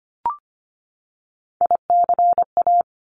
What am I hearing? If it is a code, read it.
Morse code, heard as a keyed tone, about 25 wpm, 700 Hz. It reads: ICA
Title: Top Three Letter Combinations in Words 25wpm